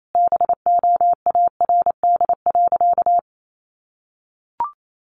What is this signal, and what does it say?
Morse code, heard as a keyed tone, about 28 wpm, 700 Hz. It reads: BOARD.
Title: Sentences from Top 1100 Words 28wpm